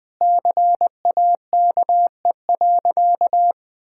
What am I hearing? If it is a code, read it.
Morse code, heard as a keyed tone, about 21 wpm, 700 Hz. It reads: CAKE.
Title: Sentences with 2-Word Phrases - Spoken Beginning - 20wpm